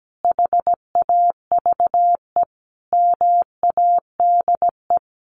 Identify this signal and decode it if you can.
Morse code, heard as a keyed tone, about 17 wpm, 700 Hz. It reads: HAVE MADE
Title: Sentences from Top 1400 Words 17wpm